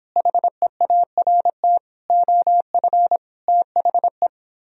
Morse code, heard as a keyed tone, about 29 wpm, 700 Hz. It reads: HEART OF THE